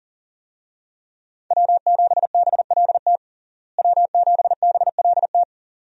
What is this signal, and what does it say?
Morse code, heard as a keyed tone, about 40 wpm, 700 Hz. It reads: W7BLT W7BLT
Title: USA Call Signs - 5 Characters - Repeated 2x - 40wpm